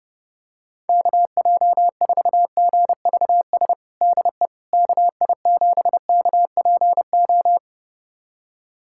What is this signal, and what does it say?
Morse code, heard as a keyed tone, about 30 wpm, 700 Hz. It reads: KJ4GVS DE KI7KPO
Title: USA Callsign DE Callsign - 6 Characters - 30wpm